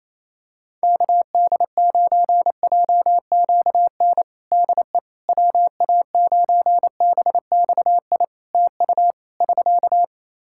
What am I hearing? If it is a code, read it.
Morse code, heard as a keyed tone, about 27 wpm, 700 Hz. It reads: KD9JQN DE WA9BXI TU <SK>
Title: Practice QSO - Exchange 4 - Ending - 28wpm